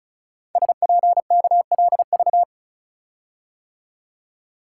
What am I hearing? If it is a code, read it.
Morse code, heard as a keyed tone, about 36 wpm, 690 Hz. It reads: SPKLV
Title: Code Groups - 5 Letters - 35wpm